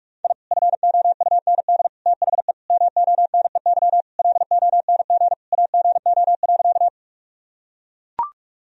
Morse code, heard as a keyed tone, about 42 wpm, 700 Hz. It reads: I FOUND THE MONEY LONG AGO.
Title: Sentences from Top 1000 Words - Review Part 3 - 45wpm